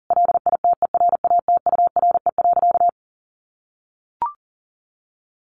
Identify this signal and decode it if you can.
Morse code, heard as a keyed tone, about 40 wpm, 700 Hz. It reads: LITERATURE.
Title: Sentences from Top 900 Words 40wpm